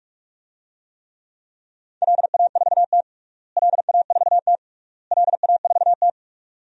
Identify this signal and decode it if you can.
Morse code, heard as a keyed tone, about 46 wpm, 700 Hz. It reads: LA4T LA4T LA4T
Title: DXCC-100 Call Signs - 4 Characters - Repeated 3x - 45wpm